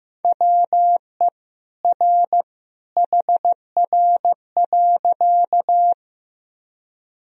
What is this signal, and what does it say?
Morse code, heard as a keyed tone, about 15 wpm, 700 Hz. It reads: WE R HR.
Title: Sentences from Top 200 Words and Abbreviations 15wpm